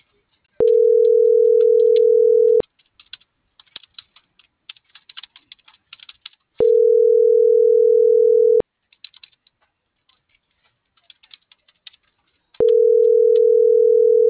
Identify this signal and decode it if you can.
Ringback tone (North America), 440 + 480 Hz.